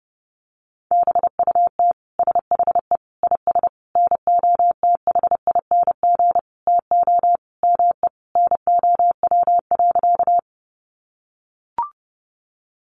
Morse code, heard as a keyed tone, about 30 wpm, 700 Hz. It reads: BUT SHE IS NOTHING TO ME NOW.
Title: Sentences from Top 500 Words 30wpm